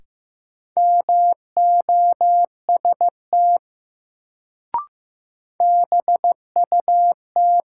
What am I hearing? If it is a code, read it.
Morse code, heard as a keyed tone, about 15 wpm, 700 Hz. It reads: MOST BUT
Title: Sets of 2 Words - New Word Drill - 15wpm